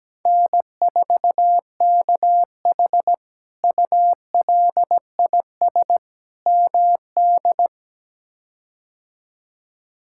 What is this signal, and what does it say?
Morse code, heard as a keyed tone, about 17 wpm, 700 Hz. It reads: N4KH ULIS MD
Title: K1USN SST QSO Practice - 17wpm